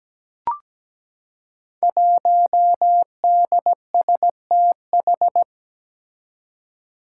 Morse code, heard as a keyed tone, about 17 wpm, 700 Hz. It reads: 1DSTH
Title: Five Challenging Letter-Number Combinations 17wpm